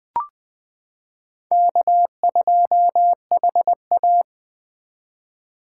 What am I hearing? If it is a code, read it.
Morse code, heard as a keyed tone, about 20 wpm, 700 Hz. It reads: K2HA
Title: 2023 POTA Hunter Call Signs - 20wpm